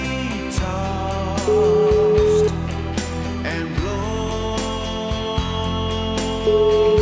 Ringback tone (Europe, 425 Hz), with noise.